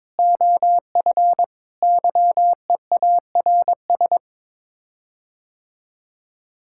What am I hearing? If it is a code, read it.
Morse code, heard as a keyed tone, about 22 wpm, 700 Hz. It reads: OF YEARS
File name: practice / 22wpm / Sets of 6 Words - 22wpm